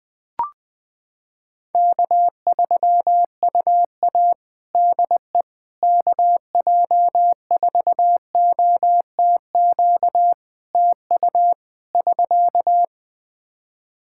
Morse code, heard as a keyed tone, about 20 wpm, 700 Hz. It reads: K3UA DE KJ4OTQ TU <SK>
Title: Practice QSO - Exchange 4 - Ending - 20wpm